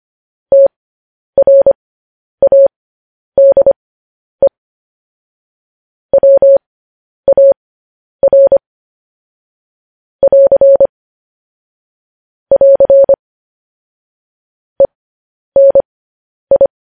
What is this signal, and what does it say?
Morse code, heard as a keyed tone, about 25 wpm, spaced as for 10 wpm, 550 Hz. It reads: TRADE WAR <AR> <AR> ENI
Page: news headlines in morse code at 10 wpm